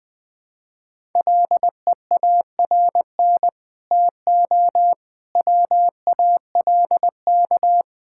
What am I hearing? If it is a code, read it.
Morse code, heard as a keyed tone, about 19 wpm, 700 Hz. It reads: LEARN TO WALK
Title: Sets of 9 Words - 20wpm